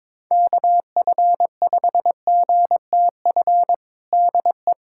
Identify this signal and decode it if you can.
Morse code, heard as a keyed tone, about 22 wpm, 700 Hz. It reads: KF5GTF DE
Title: USA Callsign DE Callsign 22wpm